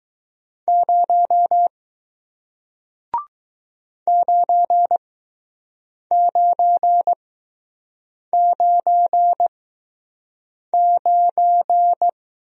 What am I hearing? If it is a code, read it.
Morse code, heard as a keyed tone, about 20 wpm, spaced as for 6 wpm, 700 Hz. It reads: 0 9999